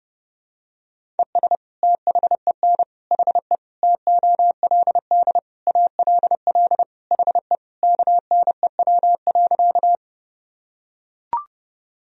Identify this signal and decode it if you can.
Morse code, heard as a keyed tone, about 32 wpm, 700 Hz. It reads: ES THEN HE TOLD ALL HE KNEW.
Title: Sentences from Top 400 Words and Abbreviations 30wpm